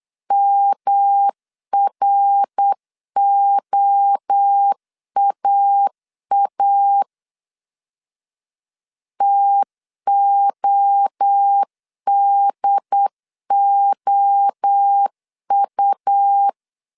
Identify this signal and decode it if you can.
Morse code, heard as a keyed tone, about 8 wpm, 800 Hz. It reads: MROAA TODOU
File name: CW / DET / 1 3 1500C S C 8wpm 4214